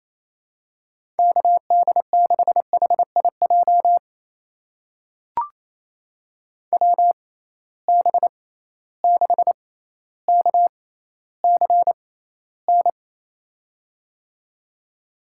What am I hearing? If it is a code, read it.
Morse code, heard as a keyed tone, about 27 wpm, spaced as for 14 wpm, 700 Hz. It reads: KD6HIJ WB6KCN
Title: USA Call Signs - 6 Characters - 6x Character Spacing - 28wpm